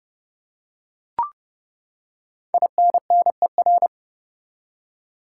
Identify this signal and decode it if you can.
Morse code, heard as a keyed tone, about 30 wpm, 700 Hz. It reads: INNER
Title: Five Letter Words - 30wpm